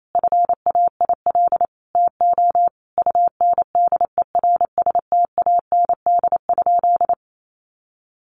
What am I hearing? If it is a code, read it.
Morse code, heard as a keyed tone, about 28 wpm, 700 Hz. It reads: FAIL TO UNDERSTAND?
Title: Sentences from Top 1100 Words 28wpm